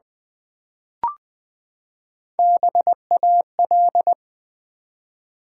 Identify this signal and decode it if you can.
Morse code, heard as a keyed tone, about 20 wpm, 700 Hz. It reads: BAL
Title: Build A Word - 3 Letters At A Time - 8-Letter Words - 20wpm